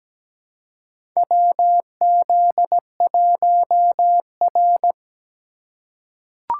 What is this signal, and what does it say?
Morse code, heard as a keyed tone, about 17 wpm, 700 Hz. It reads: WZ1R
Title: USA Call Signs - 3x Character Spacing - 17wpm